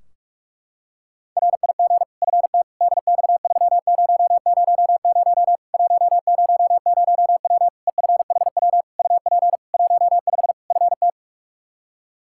Morse code, heard as a keyed tone, about 46 wpm, 700 Hz. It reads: RIG FT DX3000 100W EFHW UP 15 FT
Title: Practice QSO - Exchange 2 - Info - 45wpm